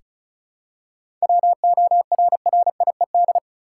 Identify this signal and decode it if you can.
Morse code, heard as a keyed tone, about 34 wpm, 700 Hz. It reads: WORRIED
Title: Seven Letter Words - 35wpm